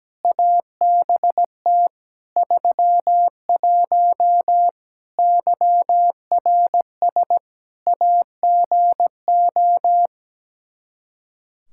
Morse code, heard as a keyed tone, about 17 wpm, 700 Hz. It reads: ABT 31 YRS AGO
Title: K1USN SST QSO Practice 17wpm